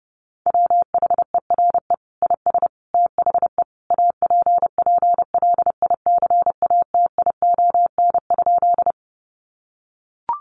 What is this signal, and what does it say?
Morse code, heard as a keyed tone, about 28 wpm, 700 Hz. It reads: WHERE IS THE APPLICATION?
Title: Sentences from Top 1200 Words 30wpm